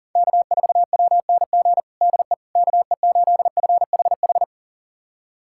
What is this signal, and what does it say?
Morse code, heard as a keyed tone, about 40 wpm, 700 Hz. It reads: K4WNG DE KE8FHH
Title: USA Callsign DE Callsign 40wpm